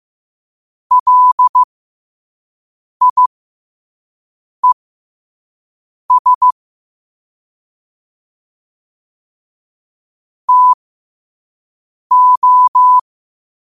Morse code, heard as a keyed tone, about 15 wpm, spaced as for 5 wpm, 1000 Hz. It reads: LIES TO
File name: qotdfiles / 5wpm